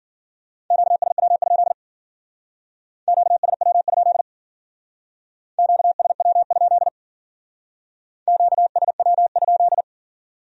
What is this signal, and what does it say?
Morse code, heard as a keyed tone, about 51 wpm, 700 Hz. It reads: QSW? QSW? QSW? QSW?